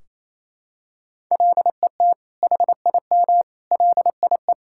Morse code, heard as a keyed tone, about 28 wpm, 700 Hz. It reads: LET HIM LIE
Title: Sentences from Top 1600 Words 28wpm